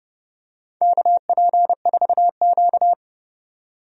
Morse code, heard as a keyed tone, about 30 wpm, 700 Hz. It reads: KP4Q
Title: DXCC-100 Call Signs - 4 Characters - 30wpm